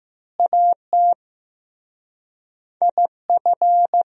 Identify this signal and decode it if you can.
Morse code, heard as a keyed tone, about 15 wpm, 700 Hz. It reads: AT IF
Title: Speed-Racing - Sets of 4 Words 15wpm